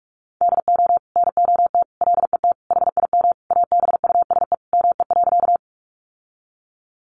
Morse code, heard as a keyed tone, about 44 wpm, 700 Hz. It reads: DO NOT LET HIM ABUSE ME.
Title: Sentences from Top 1300 Words 45wpm